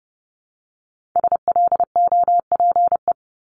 Morse code, heard as a keyed tone, about 30 wpm, 700 Hz. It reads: SLOPE
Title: Multi-letter Phonemes Repeated 3x 30wpm